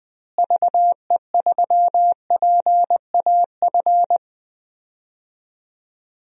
Morse code, heard as a keyed tone, about 20 wpm, 700 Hz. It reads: VE3PAF
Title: DXCC-100 Call Signs - 6 Characters - Repeated 3x - 20wpm